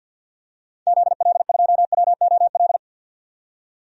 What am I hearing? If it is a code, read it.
Morse code, heard as a keyed tone, about 50 wpm, 700 Hz. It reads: ZL2WOL
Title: DXCC-100 Call Signs - 6 Characters - 50wpm